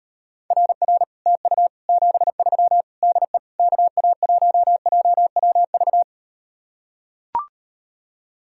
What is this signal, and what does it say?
Morse code, heard as a keyed tone, about 38 wpm, 700 Hz. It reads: RR TU 73 DE KA1JWV